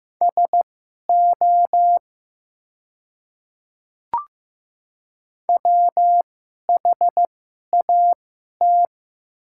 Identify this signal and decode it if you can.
Morse code, heard as a keyed tone, about 15 wpm, spaced as for 11 wpm, 700 Hz. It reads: SO WHAT
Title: 2x Character Spacing - Top 200 Words - 15wpm